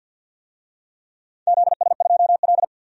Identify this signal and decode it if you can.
Morse code, heard as a keyed tone, about 51 wpm, 700 Hz. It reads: ZS2L
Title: DXCC-100 Call Signs - 4 Characters - 50wpm